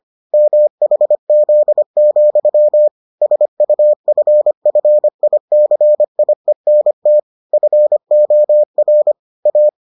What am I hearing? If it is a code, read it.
Morse code, heard as a keyed tone, about 25 wpm, 600 Hz. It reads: MHZ, SUFFICIENT FOR A